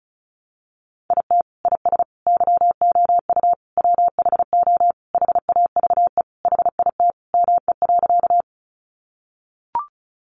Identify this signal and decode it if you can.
Morse code, heard as a keyed tone, about 34 wpm, 700 Hz. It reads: IT IS YOU WHO HAVE HIT ME.
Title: Sentences from Top 1200 Words 35wpm